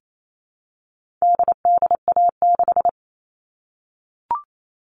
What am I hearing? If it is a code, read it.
Morse code, heard as a keyed tone, about 28 wpm, 700 Hz. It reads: DDA6
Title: All Cut Numbers - 4-Digit 28wpm